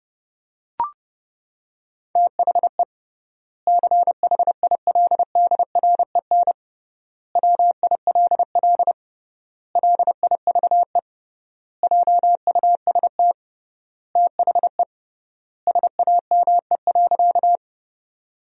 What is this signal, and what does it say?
Morse code, heard as a keyed tone, about 30 wpm, 700 Hz. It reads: THE CHILDREN WILL LIVE JUST THE SAME.
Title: 3x Word-Spacing - Sentences from Top 300 Words - 30wpm